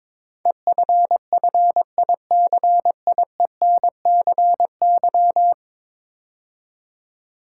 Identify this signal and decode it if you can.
Morse code, heard as a keyed tone, about 22 wpm, 700 Hz. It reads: EFFICIENCY
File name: practice / 22wpm / 10 Letter Words - First 3 Letters Given - 22wpm